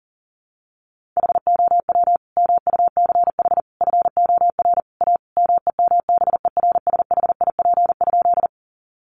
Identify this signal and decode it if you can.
Morse code, heard as a keyed tone, about 40 wpm, 700 Hz. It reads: HOW MUCH FOR A MEMBERSHIP?
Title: Sentences from Top 1400 Words 40wpm